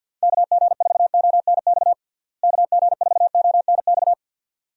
Morse code, heard as a keyed tone, about 51 wpm, 700 Hz. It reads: KG4ONX KG4ONX
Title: USA Call Signs - 6 Characters - Repeated 2x - 50wpm